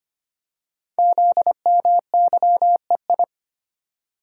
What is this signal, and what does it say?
Morse code, heard as a keyed tone, about 25 wpm, 700 Hz. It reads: ZMYEI
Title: Code Groups - 5 Letters - 25wpm